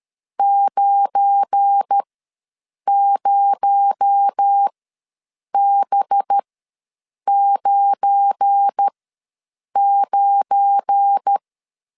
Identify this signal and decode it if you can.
Morse code, heard as a keyed tone, about 13 wpm, spaced as for 7 wpm, 800 Hz. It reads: 90B99
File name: LEZIONI / 7 GF9Z / 7 GF9 ws 12wpm 3344